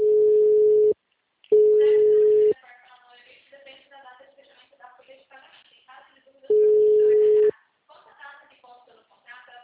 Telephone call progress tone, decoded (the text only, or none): ringback tone